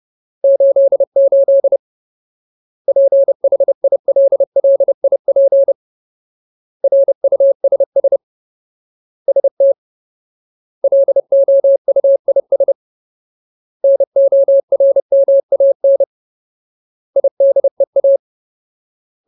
Morse code, heard as a keyed tone, about 30 wpm, 550 Hz. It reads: 88 PHILLIP RUSS ST LOUIS NORMAN IDEA
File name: Ultimate Word List / 30WPM Ultimate